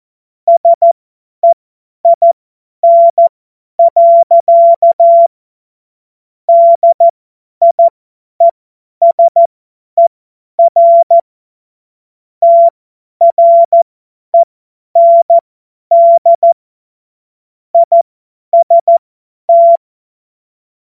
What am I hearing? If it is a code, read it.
Morse code, heard as a keyed tone, about 14 wpm, spaced as for 10 wpm, 680 Hz. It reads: SEIN. DIESER TREND IST
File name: Morsekurs / Witze / Witz 22. Juni 2021, 50 F70